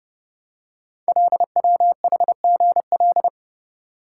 Morse code, heard as a keyed tone, about 30 wpm, 700 Hz. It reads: LWHGL
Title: Code Groups - 5 Letter:Numbers - 30wpm